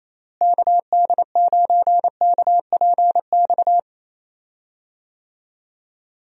Morse code, heard as a keyed tone, about 28 wpm, 700 Hz. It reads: KD9KPX
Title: USA Call Signs - 6 Characters - Repeated 3x - 28wpm